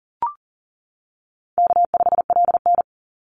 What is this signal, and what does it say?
Morse code, heard as a keyed tone, about 40 wpm, 700 Hz. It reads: K5LN
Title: Easy Call Signs - 40wpm